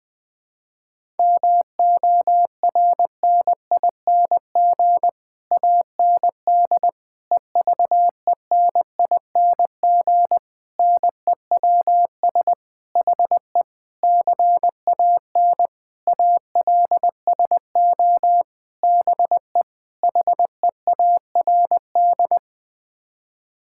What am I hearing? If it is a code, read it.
Morse code, heard as a keyed tone, about 20 wpm, 700 Hz. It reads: MORNING AND EVENING NEWS HE CAN ALSO BE HEARD
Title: Sets of 9 Words - 20wpm